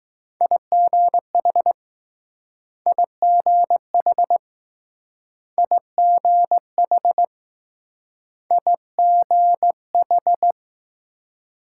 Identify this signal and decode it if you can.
Morse code, heard as a keyed tone, about 20 wpm, 700 Hz. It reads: IGH IGH IGH I G H